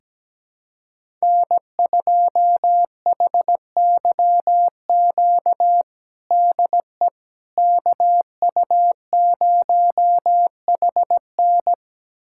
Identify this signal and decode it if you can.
Morse code, heard as a keyed tone, about 17 wpm, 700 Hz. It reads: N2HYQ DE KU0HN
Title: USA Callsign DE Callsign - 5 Characters - 17wpm